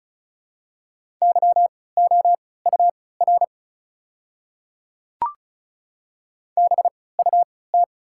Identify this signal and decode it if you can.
Morse code, heard as a keyed tone, about 36 wpm, spaced as for 20 wpm, 700 Hz. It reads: YOUR BUT
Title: 3x Character Spacing - Top 200 Words - 35wpm